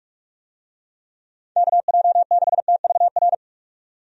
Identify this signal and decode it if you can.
Morse code, heard as a keyed tone, about 46 wpm, 700 Hz. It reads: KJ6TVR
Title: USA Call Signs - 6 Characters - 45wpm